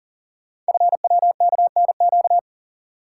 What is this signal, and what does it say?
Morse code, heard as a keyed tone, about 42 wpm, 700 Hz. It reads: FWKNQ
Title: Code Groups - 5 Letters - 40wpm